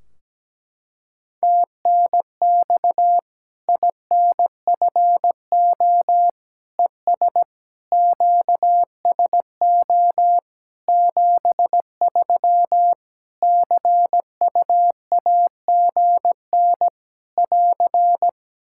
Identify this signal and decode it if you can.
Morse code, heard as a keyed tone, about 17 wpm, 700 Hz. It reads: TNX INFO ES QSO 73 CUAGN <AR>
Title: Practice QSO - Exchange 4 - Ending - 17wpm